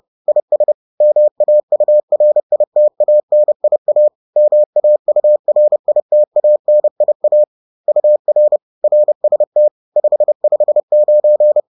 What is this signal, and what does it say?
Morse code, heard as a keyed tone, about 30 wpm, 600 Hz. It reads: IS MAURITANIA MAURITANIA UR RST 559
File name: 30wpm-7